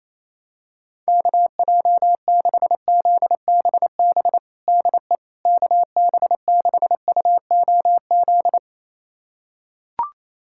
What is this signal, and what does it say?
Morse code, heard as a keyed tone, about 28 wpm, 700 Hz. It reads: KJ6ZBB DE KB6UOZ
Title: USA Callsign DE Callsign 28wpm